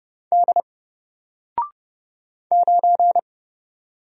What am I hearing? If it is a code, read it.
Morse code, heard as a keyed tone, about 30 wpm, 700 Hz. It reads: D 9
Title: Single Letter-Number 30wpm